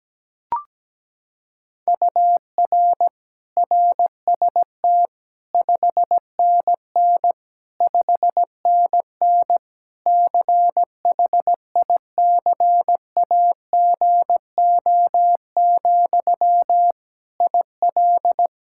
Morse code, heard as a keyed tone, about 17 wpm, 700 Hz. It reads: UR RST 5NN 5NN CHICAGO, IL
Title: K1USN SST QSO Practice 17wpm